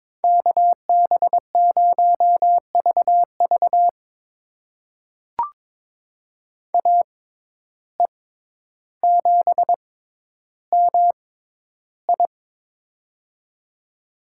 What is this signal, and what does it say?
Morse code, heard as a keyed tone, about 22 wpm, spaced as for 11 wpm, 700 Hz. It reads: KB0VV AE7MI
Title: USA Call Signs - 5 Characters - 6x Character Spacing - 22wpm